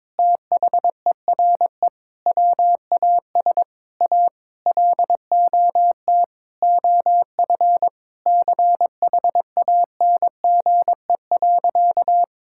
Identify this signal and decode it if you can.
Morse code, heard as a keyed tone, about 22 wpm, 700 Hz. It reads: THERE WAS A LOT OF CHANGE.
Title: Sentences from Top 1000 Words - Review Part 5 - 22wpm